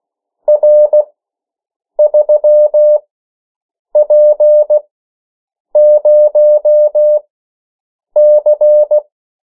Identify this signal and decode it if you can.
Morse code, heard as a keyed tone, about 16 wpm, spaced as for 7 wpm, 600 Hz. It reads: R3P0C
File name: morse mp3 / 14 urok 13